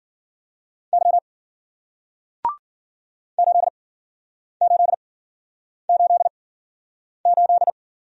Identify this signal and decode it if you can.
Morse code, heard as a keyed tone, about 51 wpm, spaced as for 9 wpm, 700 Hz. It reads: Y 8888